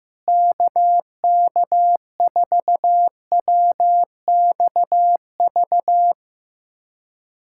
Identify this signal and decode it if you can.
Morse code, heard as a keyed tone, about 15 wpm, 700 Hz. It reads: KK4WXV
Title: USA Callsign DE Callsign - 6 Characters - 15wpm